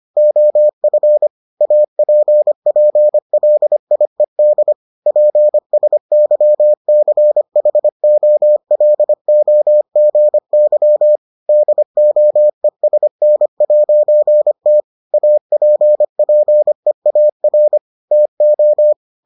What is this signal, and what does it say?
Morse code, heard as a keyed tone, about 25 wpm, 600 Hz. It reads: OF APPLIED PSYCHOLOGY DOESN'T APPEAR TO